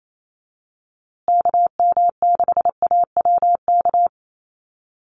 Morse code, heard as a keyed tone, about 28 wpm, 700 Hz. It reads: KM6AWK
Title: USA Call Signs Repeated 2x 28wpm